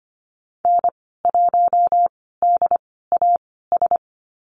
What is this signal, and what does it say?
Morse code, heard as a keyed tone, about 25 wpm, spaced as for 16 wpm, 700 Hz. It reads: N1DAS
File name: Lesson 15g - Number 5 - Call Signs 25wpm